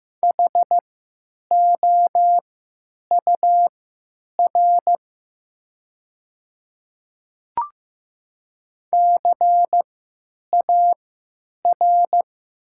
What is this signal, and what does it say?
Morse code, heard as a keyed tone, about 15 wpm, spaced as for 9 wpm, 700 Hz. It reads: HOUR CAR
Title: 3x Character Spacing - Top 500 Words - 15wpm